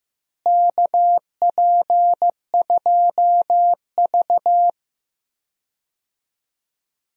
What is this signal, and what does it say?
Morse code, heard as a keyed tone, about 15 wpm, 700 Hz. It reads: KP2V
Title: DXCC-100 Call Signs - 4 Characters - Repeated 3x - 15wpm